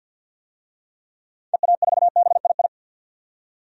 Morse code, heard as a keyed tone, about 52 wpm, 700 Hz. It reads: EA4BII
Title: DXCC-100 Call Signs - 6 Characters - 50wpm